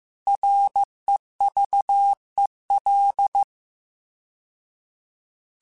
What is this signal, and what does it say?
Morse code, heard as a keyed tone, about 15 wpm, 800 Hz. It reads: REVEL